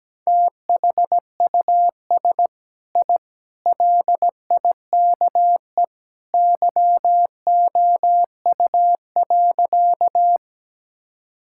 Morse code, heard as a keyed tone, about 17 wpm, 700 Hz. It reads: THUS I LIKE YOU.
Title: Sentences from Top 1000 Words - Review Part 5 - 17wpm